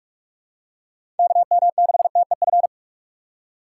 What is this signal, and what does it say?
Morse code, heard as a keyed tone, about 47 wpm, 700 Hz. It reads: KM6TEF
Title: USA Call Signs - 6 Characters - 4x Character Spacing - 45wpm